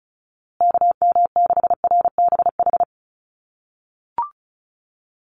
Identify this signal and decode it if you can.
Morse code, heard as a keyed tone, about 34 wpm, 700 Hz. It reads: KM6RBH
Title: USA Call Signs Repeated 3x 35wpm